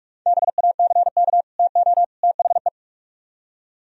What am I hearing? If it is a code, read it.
Morse code, heard as a keyed tone, about 45 wpm, 700 Hz. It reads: BACK TO THE